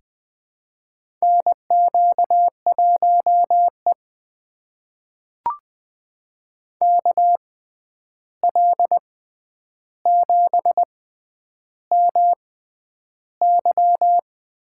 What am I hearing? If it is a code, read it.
Morse code, heard as a keyed tone, about 20 wpm, spaced as for 10 wpm, 700 Hz. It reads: NQ1E KL7MY
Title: USA Call Signs - 6x Character Spacing - 20wpm